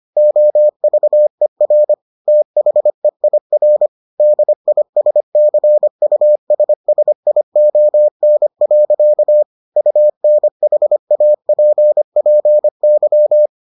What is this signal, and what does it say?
Morse code, heard as a keyed tone, about 25 wpm, 600 Hz. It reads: OVER THEIR DISCUSSION. UNHAPPY